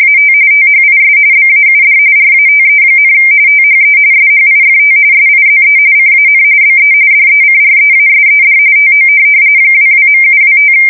RTTY, about 45.45 baud. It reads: AT THIS BAUD RATE, THE SHIFT SIZE CAN BE DECREASED TO 22.725
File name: fsk170 / teletype-RTTY